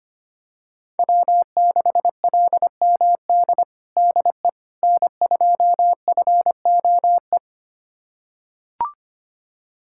Morse code, heard as a keyed tone, about 25 wpm, 700 Hz. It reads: W6LMD DE N2FOE
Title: USA Callsign DE Callsign - 5 Characters - 25wpm